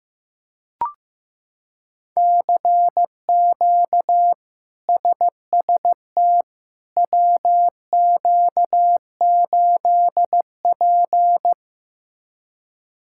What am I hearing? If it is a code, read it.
Morse code, heard as a keyed tone, about 15 wpm, 700 Hz. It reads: CQ SST WQ8P
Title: K1USN SST QSO Practice - 15wpm